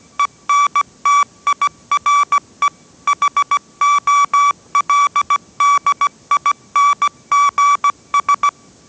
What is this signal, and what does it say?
Morse code, heard as a keyed tone, about 17 wpm, 1150 Hz. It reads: ENTIRE HOLDINGS
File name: 18wpm